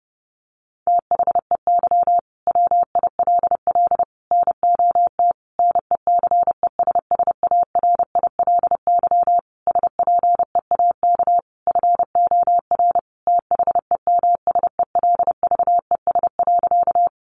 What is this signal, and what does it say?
Morse code, heard as a keyed tone, about 30 wpm, 700 Hz. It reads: THEY WILL NOT NECESSARILY SPEAK FOR THEMSELVES.
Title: Sentences from Top 1400 Words 30wpm